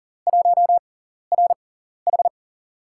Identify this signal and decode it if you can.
Morse code, heard as a keyed tone, about 40 wpm, spaced as for 14 wpm, 690 Hz. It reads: 1RH
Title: USA Call Signs - 5 Characters - 6x Character Spacing - 40wpm